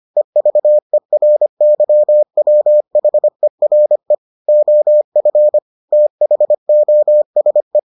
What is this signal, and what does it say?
Morse code, heard as a keyed tone, about 25 wpm, 600 Hz. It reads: EVERYWHERE OF THOSE